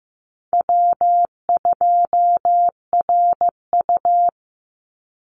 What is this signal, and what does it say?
Morse code, heard as a keyed tone, about 15 wpm, 700 Hz. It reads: W2RU
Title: Easy Call Signs - 15wpm